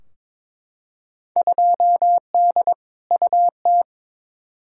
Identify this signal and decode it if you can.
Morse code, heard as a keyed tone, about 21 wpm, 700 Hz. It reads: 2D UT
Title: Field Day Exchanges Encore - 22wpm